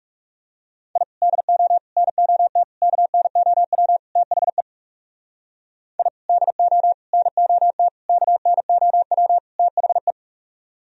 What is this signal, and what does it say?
Morse code, heard as a keyed tone, about 37 wpm, 700 Hz. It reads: I DO NOT KNOW THE I DO NOT KNOW THE
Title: Speed-Racing - Sets of 5 Words 30wpm